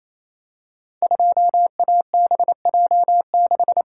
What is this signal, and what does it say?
Morse code, heard as a keyed tone, about 27 wpm, 700 Hz. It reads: 2ABJ6
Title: Code Groups - 5 Letter:Numbers - 28wpm